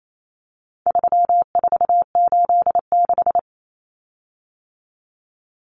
Morse code, heard as a keyed tone, about 28 wpm, 700 Hz. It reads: 3486
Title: Common Cut Numbers - 4-Digit 28wpm